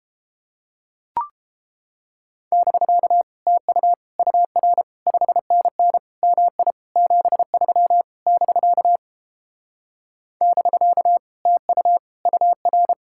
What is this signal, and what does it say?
Morse code, heard as a keyed tone, about 29 wpm, 700 Hz. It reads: <BK> TU UR 5NN MI 73 <BK> <BK> TU UR